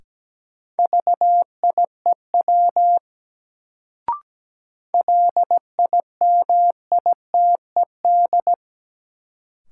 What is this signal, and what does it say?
Morse code, heard as a keyed tone, about 17 wpm, 700 Hz. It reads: VIEW LIMITED
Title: Sentences from Top 900 Words - New Word Drill - 17wpm